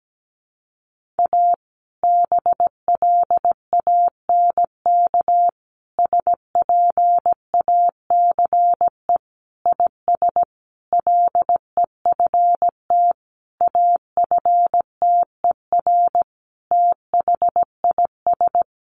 Morse code, heard as a keyed tone, about 17 wpm, 700 Hz. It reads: A BLANK SPACE IS LEFT AFTER THIS
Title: Sentences from Top 1500 Words 17wpm